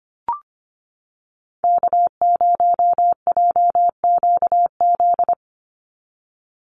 Morse code, heard as a keyed tone, about 25 wpm, 700 Hz. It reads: K0JQZ
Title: NA SOTA Chaser Callsigns 25wpm